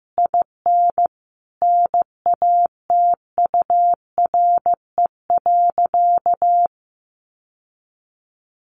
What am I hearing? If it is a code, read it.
Morse code, heard as a keyed tone, about 15 wpm, 700 Hz. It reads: IN NATURE.
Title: Sentences from Top 1400 Words 15wpm